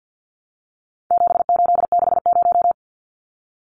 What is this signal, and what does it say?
Morse code, heard as a keyed tone, about 50 wpm, 700 Hz. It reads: 7860
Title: Numbers - 4-Digit 50wpm